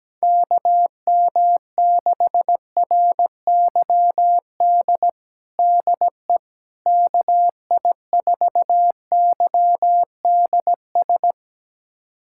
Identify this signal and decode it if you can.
Morse code, heard as a keyed tone, about 17 wpm, 700 Hz. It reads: KM6RYD DE KI4YDS
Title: USA Callsign DE Callsign - 6 Characters - 17wpm